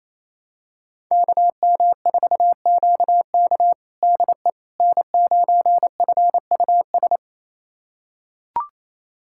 Morse code, heard as a keyed tone, about 27 wpm, 700 Hz. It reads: KM4QK DE N9FUS
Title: USA Callsign DE Callsign - 5 Characters - 28wpm